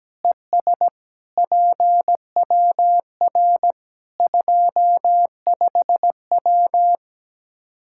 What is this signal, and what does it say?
Morse code, heard as a keyed tone, about 17 wpm, 700 Hz. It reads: ES PWR 25W
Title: Practice QSO - Exchange 2 - Info - 17wpm